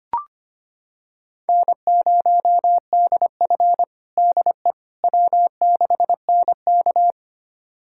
Morse code, heard as a keyed tone, about 25 wpm, 700 Hz. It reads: N0DF DE W6NK
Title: USA Callsign DE Callsign - 4 Characters - 25wpm